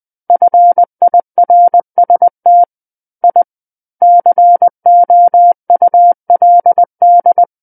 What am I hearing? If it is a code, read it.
Morse code, heard as a keyed tone, about 20 wpm, 700 Hz. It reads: FIRST I COULD
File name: 20 WPM